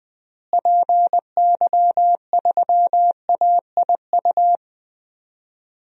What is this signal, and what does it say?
Morse code, heard as a keyed tone, about 20 wpm, 700 Hz. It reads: PY3AIU
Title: DXCC-100 Call Signs - 6 Characters - Repeated 3x - 20wpm